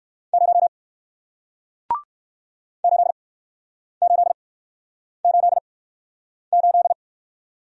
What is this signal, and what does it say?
Morse code, heard as a keyed tone, about 53 wpm, spaced as for 9 wpm, 700 Hz. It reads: 0 8888
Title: Speed-Racing Single Letter-Number 45wpm